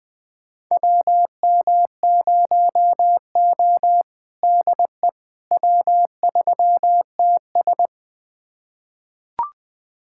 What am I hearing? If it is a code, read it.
Morse code, heard as a keyed tone, about 20 wpm, 700 Hz. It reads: WM0O DE W3TS
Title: USA Callsign DE Callsign - 4 Characters - 20wpm